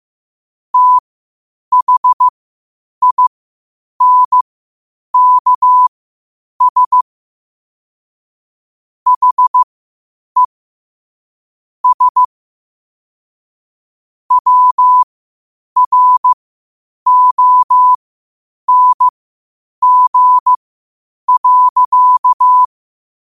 Morse code, heard as a keyed tone, about 15 wpm, spaced as for 8 wpm, 1000 Hz. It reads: THINKS HE S WRONG.
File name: qotdfiles / 8wpm